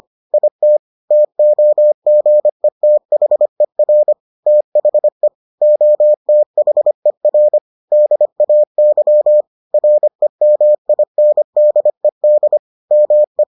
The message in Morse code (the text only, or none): IT TOGETHER THE OTHER DAY REMINDED ME